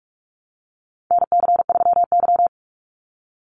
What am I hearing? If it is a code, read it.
Morse code, heard as a keyed tone, about 44 wpm, 700 Hz. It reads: NC3Y
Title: Call Signs - 45wpm